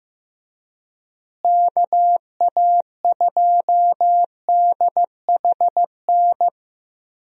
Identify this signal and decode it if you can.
Morse code, heard as a keyed tone, about 16 wpm, 700 Hz. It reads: KA2DHN
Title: USA Call Signs - 6 Characters - Repeated 2x - 15wpm